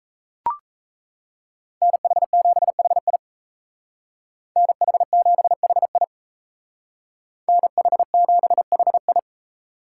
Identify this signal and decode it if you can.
Morse code, heard as a keyed tone, about 36 wpm, 700 Hz. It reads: NH7HI NH7HI NH7HI